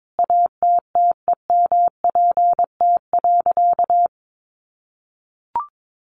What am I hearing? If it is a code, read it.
Morse code, heard as a keyed tone, about 22 wpm, 700 Hz. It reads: ATTEMPT.
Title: Sentences from Top 1000 Words 22wpm